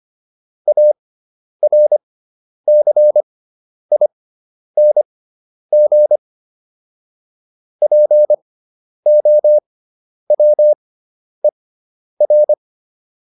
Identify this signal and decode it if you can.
Morse code, heard as a keyed tone, about 25 wpm, spaced as for 10 wpm, 600 Hz. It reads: ARCING POWER